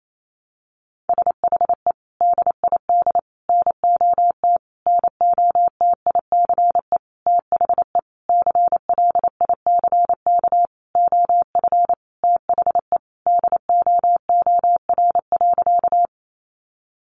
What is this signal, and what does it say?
Morse code, heard as a keyed tone, about 28 wpm, 700 Hz. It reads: SHE DID NOT NOTICE THE CLICK OF THE DOOR.
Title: Sentences from Top 1100 Words 28wpm